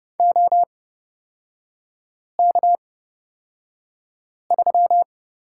Morse code, heard as a keyed tone, about 30 wpm, spaced as for 5 wpm, 700 Hz. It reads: OK3
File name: Lesson 37e - Letter x - Three Characters 30wpm